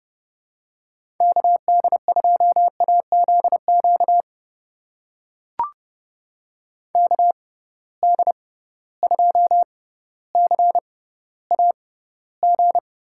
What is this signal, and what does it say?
Morse code, heard as a keyed tone, about 30 wpm, spaced as for 15 wpm, 700 Hz. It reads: KD2AZQ KD2CAG
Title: USA Call Signs - 6 Characters - 6x Character Spacing - 30wpm